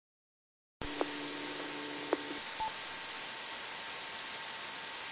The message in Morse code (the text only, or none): none